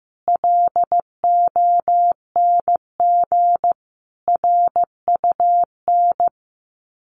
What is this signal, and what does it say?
Morse code, heard as a keyed tone, about 14 wpm, 700 Hz. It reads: LONG RUN
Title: Sets of 4 Words 15wpm